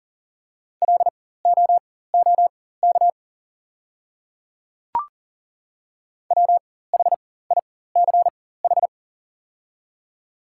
Morse code, heard as a keyed tone, about 40 wpm, spaced as for 19 wpm, 690 Hz. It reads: LOOK WHICH
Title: 4x Character Spacing - Top 100 Words - 40wpm